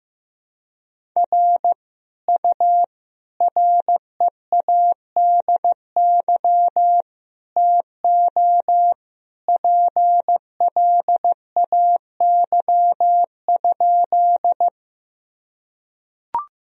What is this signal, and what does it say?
Morse code, heard as a keyed tone, about 15 wpm, 700 Hz. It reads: R U READY TO PLAY?
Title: Sentences from Top 400 Words and Abbreviations 15wpm